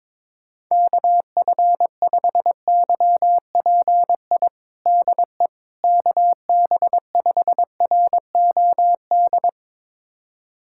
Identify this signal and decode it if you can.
Morse code, heard as a keyed tone, about 21 wpm, 700 Hz. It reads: KF5YPI DE KB5ROD
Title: USA Callsign DE Callsign 22wpm